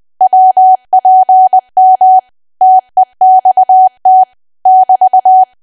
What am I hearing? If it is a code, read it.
Morse code, heard as a keyed tone, about 20 wpm, 750 Hz. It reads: WPM TEXT <BT>
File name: cwa / W1AW-Level3 / 20WPM